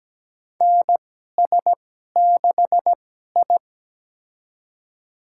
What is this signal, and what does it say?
Morse code, heard as a keyed tone, about 17 wpm, spaced as for 12 wpm, 700 Hz. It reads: NS6I
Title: USA Call Signs - 4 Characters - 2x Character Spacing - 17wpm